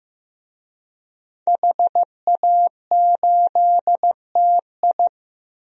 Morse code, heard as a keyed tone, about 15 wpm, 700 Hz. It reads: HA8TI
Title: DXCC-100 Call Signs - 5 Characters - Repeated 3x - 15wpm